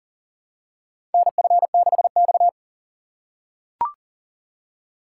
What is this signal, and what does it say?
Morse code, heard as a keyed tone, about 40 wpm, 700 Hz. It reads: NF6X
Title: USA Call Signs - 4x Character Spacing - 40wpm